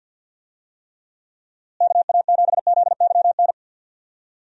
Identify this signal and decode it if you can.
Morse code, heard as a keyed tone, about 52 wpm, 700 Hz. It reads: KA7ZYN
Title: USA Call Signs - 6 Characters - Repeated 2x - 50wpm